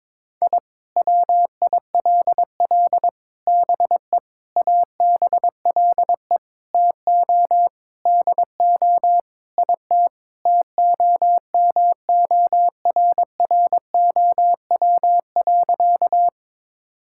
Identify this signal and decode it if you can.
Morse code, heard as a keyed tone, about 22 wpm, 700 Hz. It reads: I WILL BE ABLE TO DO IT TOMORROW.
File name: practice / 22wpm / Sentences with 4-Word Phrases - 22wpm